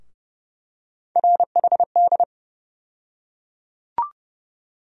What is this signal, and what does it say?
Morse code, heard as a keyed tone, about 29 wpm, 700 Hz. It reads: RHD
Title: USA Call Sign Postfixes - 3 Letters 30wpm